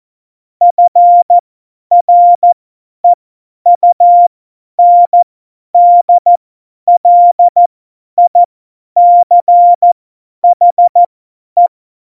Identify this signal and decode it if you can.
Morse code, heard as a keyed tone, about 14 wpm, spaced as for 10 wpm, 700 Hz. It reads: FREUNDLICHE
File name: Morsekurs / Witze / Witz 10. September 2019, 50 F70